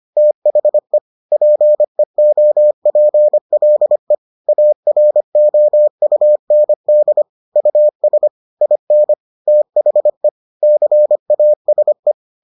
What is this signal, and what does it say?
Morse code, heard as a keyed tone, about 25 wpm, 600 Hz. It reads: THE PEOPLE AROUND US IN THE CASE